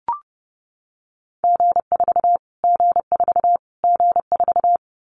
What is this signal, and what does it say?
Morse code, heard as a keyed tone, about 30 wpm, 700 Hz. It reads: G4 G4 G4
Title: Callsign Prefixes for Top 25 DXCC - Repeated 3x - 30wpm